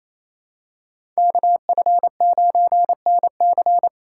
Morse code, heard as a keyed tone, about 28 wpm, 700 Hz. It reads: KF9NC
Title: USA Call Signs - 5 Characters - 2x Character Spacing - 28wpm